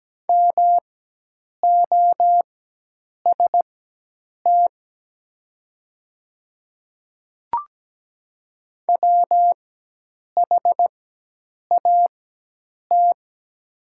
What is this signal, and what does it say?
Morse code, heard as a keyed tone, about 17 wpm, spaced as for 8 wpm, 700 Hz. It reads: MOST WHAT
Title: 4x Character Spacing - Top 200 Words - 17wpm